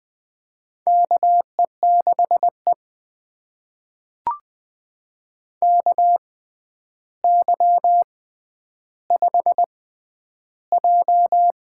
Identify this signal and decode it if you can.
Morse code, heard as a keyed tone, about 20 wpm, spaced as for 10 wpm, 700 Hz. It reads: KE6E KY5J
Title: USA Call Signs - 4 Characters - 6x Character Spacing - 20wpm